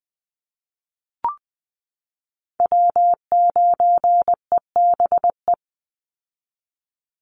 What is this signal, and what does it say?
Morse code, heard as a keyed tone, about 20 wpm, 700 Hz. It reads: W9EBE
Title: Call Signs Repeated 3x - 20wpm